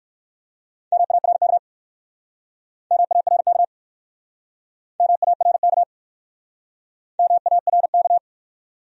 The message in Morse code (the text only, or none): MARK MARK MARK MARK